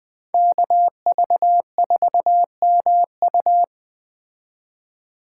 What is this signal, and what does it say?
Morse code, heard as a keyed tone, about 21 wpm, 700 Hz. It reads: KV4MU
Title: Speed Racing - USA 2x2 Call Signs 17wpm